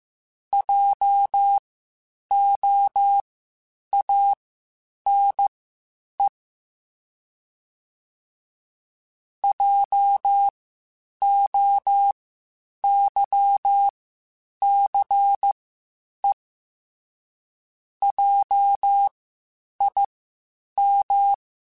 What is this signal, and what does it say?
Morse code, heard as a keyed tone, about 15 wpm, spaced as for 8 wpm, 800 Hz. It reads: JOANE JOYCE JIM
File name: Spaced Alphas / Spaced Alphas mp3s / Js sp @13